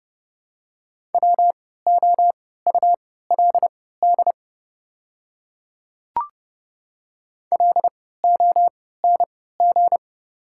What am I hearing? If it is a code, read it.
Morse code, heard as a keyed tone, about 30 wpm, spaced as for 17 wpm, 700 Hz. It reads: WOULD LONG
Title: 3x Character Spacing - Top 200 Words - 30wpm